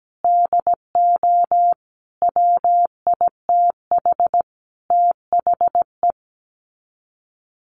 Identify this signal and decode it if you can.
Morse code, heard as a keyed tone, about 17 wpm, 700 Hz. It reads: DO WITH THE
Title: Sets of 5 Words Encore 17wpm